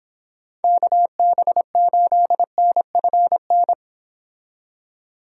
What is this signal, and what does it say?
Morse code, heard as a keyed tone, about 29 wpm, 700 Hz. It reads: KB8NFN